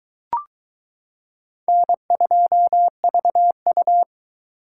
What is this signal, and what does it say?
Morse code, heard as a keyed tone, about 22 wpm, 700 Hz. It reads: N2VU